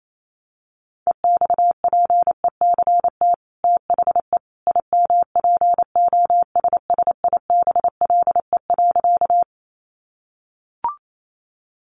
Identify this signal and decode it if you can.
Morse code, heard as a keyed tone, about 28 wpm, 700 Hz. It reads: EXPECT THE IMPOSSIBLE.
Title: Sentences from Top 1100 Words 28wpm